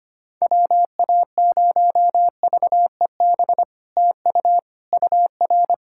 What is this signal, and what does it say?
Morse code, heard as a keyed tone, about 25 wpm, 690 Hz. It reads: WA0VEB TU UR
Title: SOTA QSO Practice 25wpm